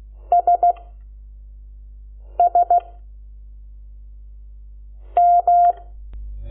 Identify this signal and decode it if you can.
Morse code, heard as a keyed tone, about 16 wpm, spaced as for 5 wpm, 700 Hz. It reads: SS M
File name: Lecciones / CLASE Nº12